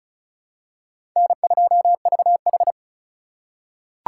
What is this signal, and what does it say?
Morse code, heard as a keyed tone, about 34 wpm, 700 Hz. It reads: N2VH
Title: USA Call Signs - 2x Character Spacing - 35wpm